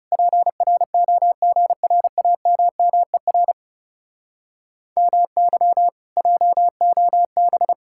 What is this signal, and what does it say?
Morse code, heard as a keyed tone, about 40 wpm, 700 Hz. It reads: PROGRAMMER MY JOB